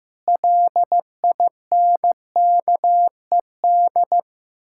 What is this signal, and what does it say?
Morse code, heard as a keyed tone, about 15 wpm, 700 Hz. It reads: LINKED
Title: Pangram Challenge v2 - 15wpm